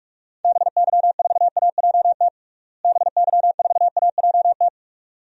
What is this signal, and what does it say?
Morse code, heard as a keyed tone, about 41 wpm, 700 Hz. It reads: BY4AJT BY4AJT
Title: DXCC-100 Call Signs - 6 Characters - Repeated 2x - 45wpm